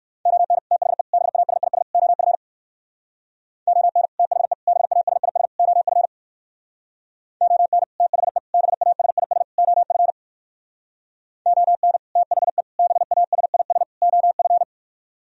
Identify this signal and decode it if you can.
Morse code, heard as a keyed tone, about 56 wpm, 700 Hz. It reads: ON THE BASIS OF ON THE BASIS OF ON THE BASIS OF ON THE BASIS OF
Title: Speed-Racing - Sets of 4 Words 45wpm